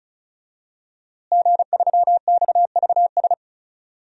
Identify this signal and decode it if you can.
Morse code, heard as a keyed tone, about 36 wpm, 700 Hz. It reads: G3XVS
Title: DXCC-100 Call Signs - 5 Characters - Repeated 2x - 35wpm